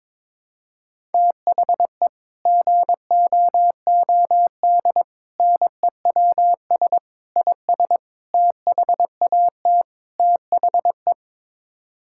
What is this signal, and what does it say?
Morse code, heard as a keyed tone, about 22 wpm, 700 Hz. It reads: THE GOOD NEWS IS THAT THE
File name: practice / 22wpm / Sets of 6 Words - 22wpm